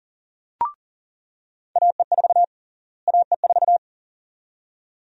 Morse code, heard as a keyed tone, about 40 wpm, 700 Hz. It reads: AE4 AE4
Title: USA Call Sign Prefixes - Repeated 2x - 40wpm